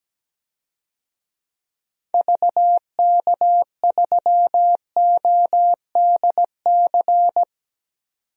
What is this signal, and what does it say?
Morse code, heard as a keyed tone, about 17 wpm, 700 Hz. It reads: VK3ODC
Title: DXCC-100 Call Signs - 6 Characters - 17wpm